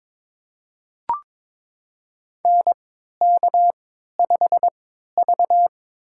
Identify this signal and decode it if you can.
Morse code, heard as a keyed tone, about 22 wpm, spaced as for 13 wpm, 700 Hz. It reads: NK5V
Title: USA Call Signs - 4 Characters - 3x Character Spacing - 22wpm